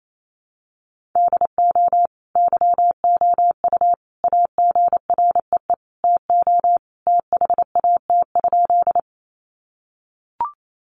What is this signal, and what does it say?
Morse code, heard as a keyed tone, about 28 wpm, 700 Hz. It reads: DO YOU AGREE TO THAT?
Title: Sentences from Top 1100 Words 28wpm